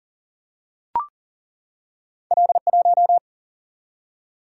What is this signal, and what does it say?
Morse code, heard as a keyed tone, about 40 wpm, 700 Hz. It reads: L 1
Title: Single-Letter Single-Number 40wpm